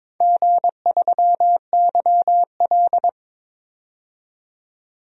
Morse code, heard as a keyed tone, about 22 wpm, 700 Hz. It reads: G3YL
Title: DXCC-100 Call Signs - 4 Characters - 22wpm